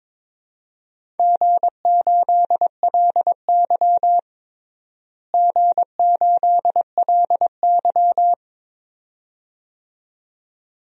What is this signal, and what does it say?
Morse code, heard as a keyed tone, about 22 wpm, 700 Hz. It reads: G8LY G8LY
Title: DXCC-100 Call Signs - 4 Characters - Repeated 2x - 22wpm